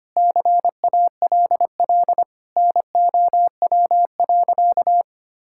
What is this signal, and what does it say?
Morse code, heard as a keyed tone, about 25 wpm, 700 Hz. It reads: CALL NOW.
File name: practice / 25wpm / Sentences with 4-Word Phrases - 25wpm